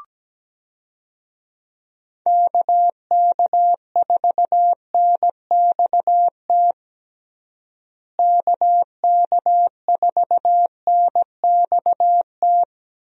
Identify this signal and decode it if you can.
Morse code, heard as a keyed tone, about 17 wpm, 700 Hz. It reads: KK4NXT KK4NXT
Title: USA Call Signs - 6 Characters - Repeated 2x - 17wpm